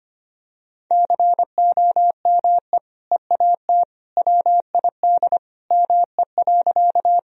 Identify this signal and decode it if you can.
Morse code, heard as a keyed tone, about 25 wpm, 700 Hz. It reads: COME EAT WID ME.
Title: Sentences from Top 300 Words and Abbreviations 25wpm